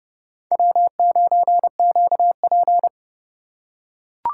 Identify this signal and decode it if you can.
Morse code, heard as a keyed tone, about 29 wpm, 700 Hz. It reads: W9QP